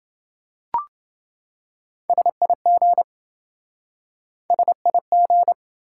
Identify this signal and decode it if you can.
Morse code, heard as a keyed tone, about 28 wpm, 700 Hz. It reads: SIG SIG